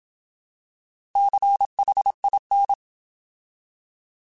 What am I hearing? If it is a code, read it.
Morse code, heard as a keyed tone, about 27 wpm, 800 Hz. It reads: CHIN
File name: Alphabetical / Alphabetical mp3s / Cs @25